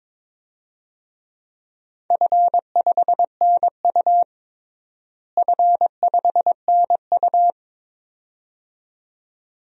Morse code, heard as a keyed tone, about 22 wpm, 700 Hz. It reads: F5NU F5NU
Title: DXCC-100 Call Signs - 4 Characters - Repeated 2x - 22wpm